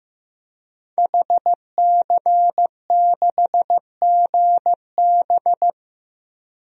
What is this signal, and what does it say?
Morse code, heard as a keyed tone, about 15 wpm, 700 Hz. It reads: HC6GB
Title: DXCC-100 Call Signs - 5 Characters - Repeated 2x - 15wpm